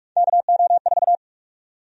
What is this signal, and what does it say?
Morse code, heard as a keyed tone, about 44 wpm, 700 Hz. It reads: KO4
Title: USA Call Sign Prefixes - 45wpm